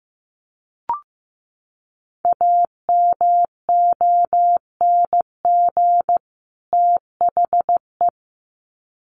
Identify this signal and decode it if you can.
Morse code, heard as a keyed tone, about 15 wpm, 700 Hz. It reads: AMONG THE
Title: Sets of 2 Words 15wpm